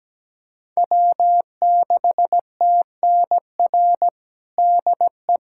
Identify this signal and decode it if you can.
Morse code, heard as a keyed tone, about 17 wpm, 700 Hz. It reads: W6TNR DE
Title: USA Callsign DE Callsign - 5 Characters - 17wpm